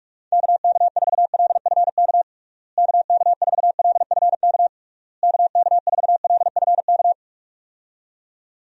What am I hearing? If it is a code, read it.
Morse code, heard as a keyed tone, about 46 wpm, 700 Hz. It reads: KK4LFK KK4LFK KK4LFK
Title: USA Call Signs - 6 Characters - Repeated 3x - 45wpm